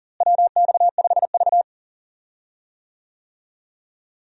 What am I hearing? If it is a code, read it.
Morse code, heard as a keyed tone, about 40 wpm, 700 Hz. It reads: WX5V
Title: USA Call Signs - 4 Characters - Repeated 2x - 40wpm